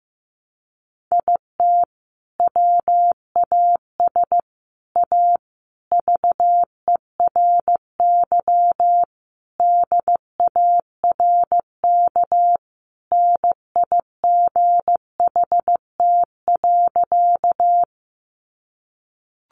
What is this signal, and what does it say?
Morse code, heard as a keyed tone, about 15 wpm, 700 Hz. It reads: IT WAS A VERY DARK NIGHT.
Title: Sentences from Top 500 Words 15wpm